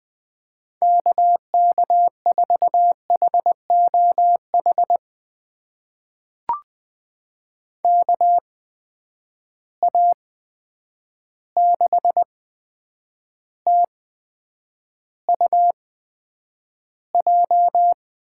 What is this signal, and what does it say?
Morse code, heard as a keyed tone, about 20 wpm, spaced as for 9 wpm, 700 Hz. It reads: KK4HOH KA6TUJ
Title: USA Call Signs - 6 Characters - 8x Character Spacing - 20wpm